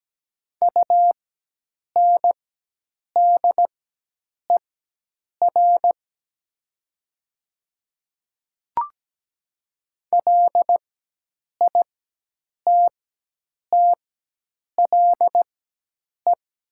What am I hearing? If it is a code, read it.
Morse code, heard as a keyed tone, about 17 wpm, spaced as for 8 wpm, 700 Hz. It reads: UNDER LITTLE
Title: 4x Character Spacing - Top 300 Words - 17wpm